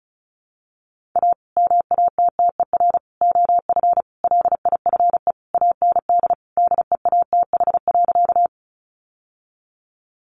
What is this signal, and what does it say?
Morse code, heard as a keyed tone, about 34 wpm, 700 Hz. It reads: A MATTER OF LIFE AND DEATH.
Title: Sentences from Top 1100 Words 35wpm